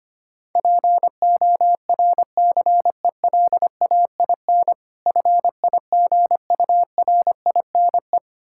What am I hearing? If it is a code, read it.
Morse code, heard as a keyed tone, about 25 wpm, 690 Hz. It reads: PORCELAIN FIGURINE
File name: practice / 25wpm / Difficult Words with Hints - 25wpm